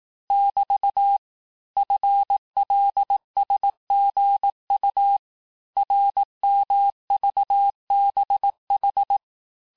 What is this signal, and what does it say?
Morse code, heard as a keyed tone, about 18 wpm, 800 Hz. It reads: <BT> FLSGU RMVBH
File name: cours-du-lundi files / cours-1594--teste-chiffre-vitesse-17mpm